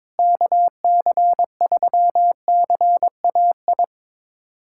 Morse code, heard as a keyed tone, about 21 wpm, 700 Hz. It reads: KC3CAI
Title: USA Callsign DE Callsign - 6 Characters - 22wpm